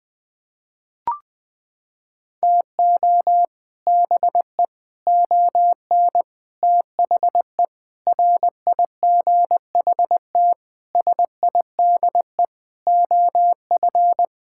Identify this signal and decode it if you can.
Morse code, heard as a keyed tone, about 21 wpm, 700 Hz. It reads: TO BE ON THE RIGHT SIDE OF
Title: Sets of 8 Words - 20wpm